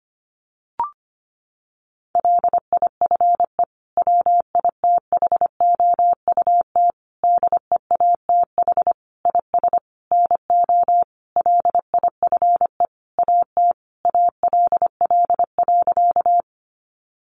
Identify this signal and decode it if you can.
Morse code, heard as a keyed tone, about 25 wpm, 700 Hz. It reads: LIFE WITHOUT DEATH IS NO LIFE AT ALL.
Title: Sentences from Top 1100 Words 25wpm